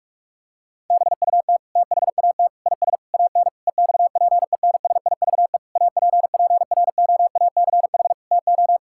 Morse code, heard as a keyed tone, about 41 wpm, 700 Hz. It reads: BUT THAT IS AN EXPENSIVE APPROACH TO
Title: Sentences from Top 1000 Words - Review Part 6 - 45wpm